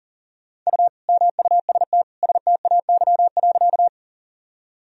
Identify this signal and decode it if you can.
Morse code, heard as a keyed tone, about 40 wpm, 700 Hz. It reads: U MUST STAY.
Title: Sentences from Top 500 Words and Abbreviations 40wpm